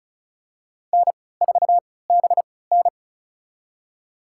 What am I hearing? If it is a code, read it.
Morse code, heard as a keyed tone, about 36 wpm, spaced as for 20 wpm, 700 Hz. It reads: N4BN
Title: USA Call Signs - 4 Characters - 3x Character Spacing - 35wpm